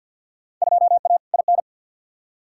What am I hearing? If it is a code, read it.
Morse code, heard as a keyed tone, about 51 wpm, 700 Hz. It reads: 2A IN
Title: Field Day Exchanges - 50wpm